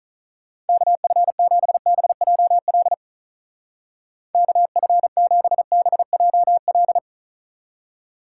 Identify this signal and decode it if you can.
Morse code, heard as a keyed tone, about 46 wpm, 700 Hz. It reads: KF7BJL KF7BJL